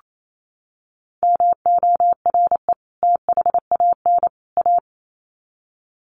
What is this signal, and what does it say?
Morse code, heard as a keyed tone, about 28 wpm, 700 Hz. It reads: MORE THAN A
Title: Sets of 3 Words 28wpm